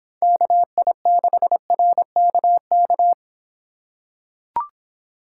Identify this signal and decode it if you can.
Morse code, heard as a keyed tone, about 26 wpm, 700 Hz. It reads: KI6RKK